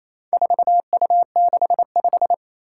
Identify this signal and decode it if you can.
Morse code, heard as a keyed tone, about 28 wpm, 700 Hz. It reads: 4U65
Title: Four Challenging Letter-Number Combinations 28wpm